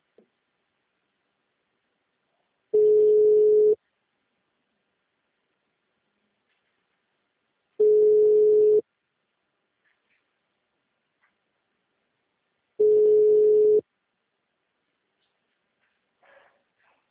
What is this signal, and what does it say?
Ringback tone (Europe, 425 Hz).